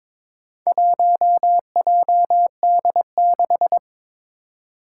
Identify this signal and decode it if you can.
Morse code, heard as a keyed tone, about 22 wpm, 700 Hz. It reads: 1JD6
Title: Four Challenging Letter-Number Combinations 22wpm